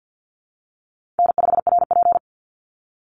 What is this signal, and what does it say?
Morse code, heard as a keyed tone, about 49 wpm, 700 Hz. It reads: N5RP
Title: Call Signs Repeated 3x - 50wpm